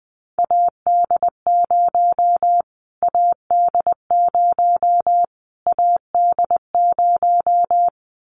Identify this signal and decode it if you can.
Morse code, heard as a keyed tone, about 20 wpm, 700 Hz. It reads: AD0 AD0 AD0
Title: Callsign Prefixes for Top 25 DXCC - Repeated 3x - 20wpm